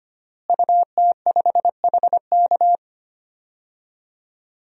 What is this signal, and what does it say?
Morse code, heard as a keyed tone, about 25 wpm, 700 Hz. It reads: UT5HK
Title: DXCC-100 Call Signs - 5 Characters - Repeated 2x - 25wpm